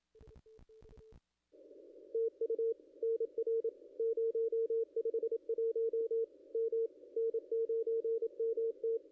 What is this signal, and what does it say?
Morse code, heard as a keyed tone, about 27 wpm, 440 Hz. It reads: TU NR 051 M N9MT
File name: contests / ss-cw-k5tr-op-n5rz -0200